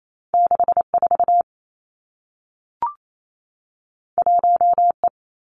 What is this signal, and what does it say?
Morse code, heard as a keyed tone, about 28 wpm, 700 Hz. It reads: 6 4 1 E
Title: Two Challenging Letter-Number Combinations 28wpm